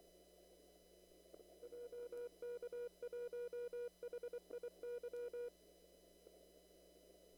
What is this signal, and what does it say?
Morse code, heard as a keyed tone, about 24 wpm, 480 Hz. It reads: JK1HIY
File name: CW / kt5j wpx cw-op-k5pi